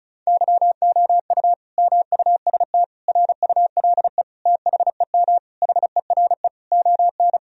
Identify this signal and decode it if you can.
Morse code, heard as a keyed tone, about 36 wpm, 700 Hz. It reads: YOU MUST RULE THEM HERE ON
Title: Sentences from Top 1000 Words - Review Part 2 - 35wpm